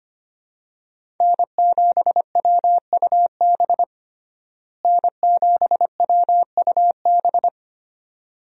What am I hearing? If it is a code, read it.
Morse code, heard as a keyed tone, about 25 wpm, 700 Hz. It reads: N7WUB N7WUB
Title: USA Call Signs - 5 Characters - Repeated 2x - 25wpm